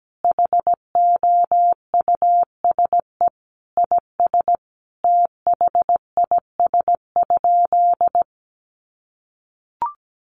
Morse code, heard as a keyed tone, about 17 wpm, 700 Hz. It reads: HOUSE IS THIS?
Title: Sentences from Top 1200 Words 17wpm